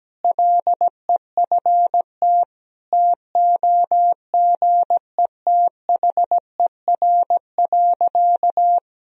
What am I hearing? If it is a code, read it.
Morse code, heard as a keyed tone, about 17 wpm, 700 Hz. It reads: LEFT TOGETHER.
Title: Sentences from Top 400 Words and Abbreviations 17wpm